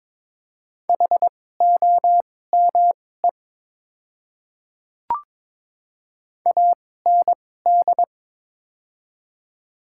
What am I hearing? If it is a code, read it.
Morse code, heard as a keyed tone, about 22 wpm, spaced as for 16 wpm, 700 Hz. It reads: HOME AND
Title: 2x Character Spacing - Top 200 Words - 22wpm